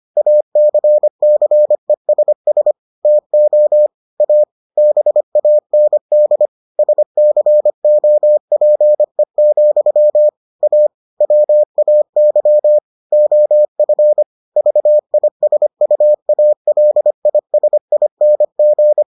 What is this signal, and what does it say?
Morse code, heard as a keyed tone, about 25 wpm, 600 Hz. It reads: ACCESS TO A BAND SCOPE, A WAY OF VISUALISING